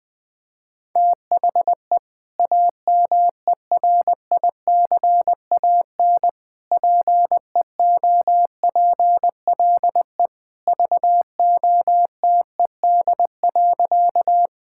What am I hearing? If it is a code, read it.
Morse code, heard as a keyed tone, about 21 wpm, 700 Hz. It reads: THE AMERICAN PEOPLE VOTED.
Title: Sentences with 3-Word Phrases - Spoken Beginning - 20wpm